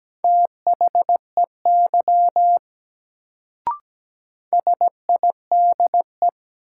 Morse code, heard as a keyed tone, about 17 wpm, 700 Hz. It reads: THEY SIDE
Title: Sets of 2 Words - New Word Drill - 17wpm